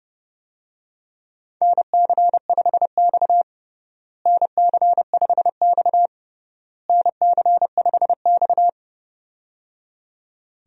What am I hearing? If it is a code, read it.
Morse code, heard as a keyed tone, about 31 wpm, 700 Hz. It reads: NC5X NC5X NC5X
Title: USA Call Signs - 4 Characters - Repeated 3x - 30wpm